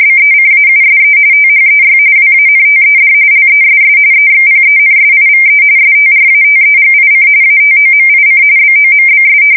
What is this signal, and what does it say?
RTTY, about 45.45 baud. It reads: BECAUSE THEY WANTED LESS EXPENSIVE STRATEGIC ATOMIC BOMBS
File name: fsk170 / usnavy5